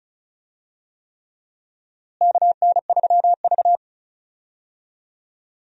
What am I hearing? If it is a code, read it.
Morse code, heard as a keyed tone, about 36 wpm, 700 Hz. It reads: KN3V
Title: USA Call Signs - 4 Characters - Repeated 3x - 35wpm